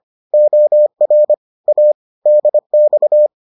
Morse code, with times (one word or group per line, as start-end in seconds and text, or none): 0.3-1.3 OR
1.7-1.9 A
2.3-3.3 DX